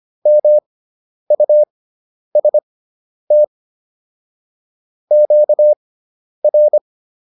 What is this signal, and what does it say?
Morse code, heard as a keyed tone, about 25 wpm, spaced as for 10 wpm, 600 Hz. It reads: MUST QR